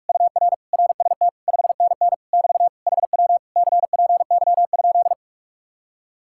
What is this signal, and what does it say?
Morse code, heard as a keyed tone, about 49 wpm, 700 Hz. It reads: UR RST 5NN <BT> HW CPY?